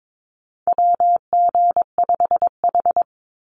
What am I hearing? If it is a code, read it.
Morse code, heard as a keyed tone, about 22 wpm, 700 Hz. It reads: WG5H
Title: Call Signs - 22wpm